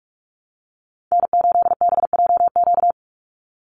Morse code, heard as a keyed tone, about 44 wpm, 700 Hz. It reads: N8BJQ
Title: Call Signs - 45wpm